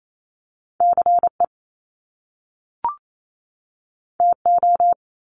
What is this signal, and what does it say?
Morse code, heard as a keyed tone, about 28 wpm, 700 Hz. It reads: CE TO
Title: Top Two Letter Combinations in Words 28wpm